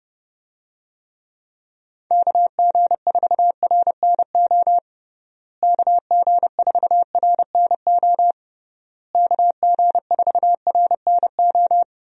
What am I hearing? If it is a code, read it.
Morse code, heard as a keyed tone, about 30 wpm, 700 Hz. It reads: KG4RNO KG4RNO KG4RNO
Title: USA Call Signs - 6 Characters - Repeated 3x - 30wpm